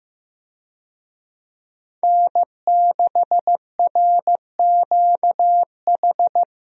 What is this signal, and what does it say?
Morse code, heard as a keyed tone, about 16 wpm, 700 Hz. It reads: N6RQH
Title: USA Call Signs - 5 Characters - Repeated 3x - 15wpm